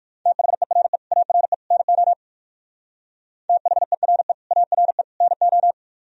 Morse code, heard as a keyed tone, about 58 wpm, 700 Hz. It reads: THERE ARE NO THERE ARE NO